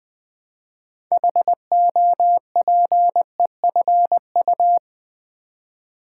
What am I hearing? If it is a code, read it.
Morse code, heard as a keyed tone, about 20 wpm, 700 Hz. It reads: HOPEFU
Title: Build A Word - 3 Letters At A Time - 9-Letter Words - 20wpm